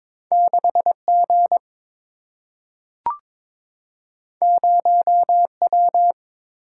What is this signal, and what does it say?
Morse code, heard as a keyed tone, about 22 wpm, 700 Hz. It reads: 6 G 0 W
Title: Single-Letter Single-Number 22wpm